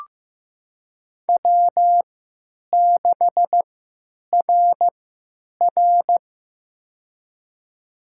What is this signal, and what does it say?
Morse code, heard as a keyed tone, about 15 wpm, spaced as for 9 wpm, 700 Hz. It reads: W6RR
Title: USA Call Signs - 4 Characters - 3x Character Spacing - 15wpm